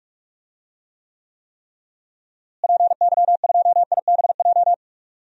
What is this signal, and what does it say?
Morse code, heard as a keyed tone, about 41 wpm, 700 Hz. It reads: PY2IBJ
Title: DXCC-100 Call Signs - 6 Characters - Repeated 3x - 45wpm